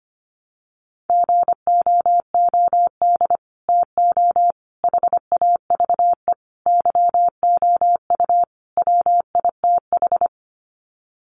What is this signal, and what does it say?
Morse code, heard as a keyed tone, about 25 wpm, 700 Hz. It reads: GOOD TO HAVE YOU WITH
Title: Sets of 5 Words Encore2 25wpm